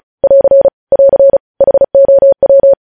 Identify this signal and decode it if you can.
Morse code, heard as a keyed tone, about 36 wpm, 550 Hz. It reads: <AR> <AR> HOW
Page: news headlines in morse code at 35 wpm